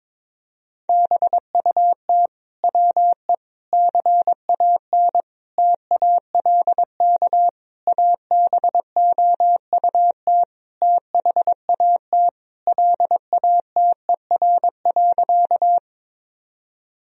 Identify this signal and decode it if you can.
Morse code, heard as a keyed tone, about 22 wpm, 700 Hz. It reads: BUT WE CAN TALK ABOUT THAT LATER.
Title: Sentences from Top 1000 Words - Review Part 4 - 22wpm